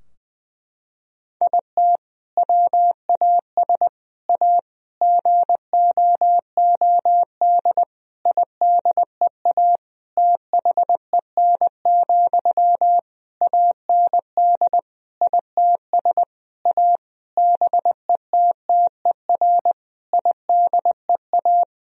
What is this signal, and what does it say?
Morse code, heard as a keyed tone, about 19 wpm, 700 Hz. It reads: IT WAS A GOOD IDEA THEN, AND ITS A BETTER IDEA
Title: Sentences from Top 1000 Words - Review Part 4 - 20wpm